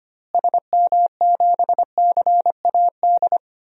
Morse code, heard as a keyed tone, about 25 wpm, 700 Hz. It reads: SM7CAD
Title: DXCC-100 Call Signs - 6 Characters - Repeated 2x - 25wpm